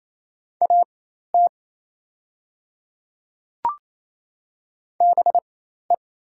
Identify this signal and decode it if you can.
Morse code, heard as a keyed tone, about 28 wpm, spaced as for 13 wpm, 700 Hz. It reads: AT BE
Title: 4x Character Spacing - Top 200 Words - 28wpm